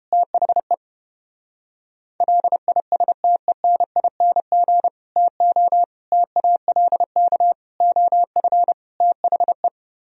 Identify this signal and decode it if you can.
Morse code, heard as a keyed tone, about 29 wpm, 700 Hz. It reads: THE LISTENING TO TALK OF THE